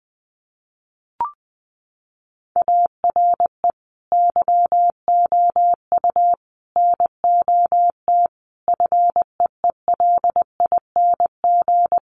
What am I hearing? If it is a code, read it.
Morse code, heard as a keyed tone, about 20 wpm, 700 Hz. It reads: ARE YOU NOT FEELING
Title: Sentences from Top 1200 Words 20wpm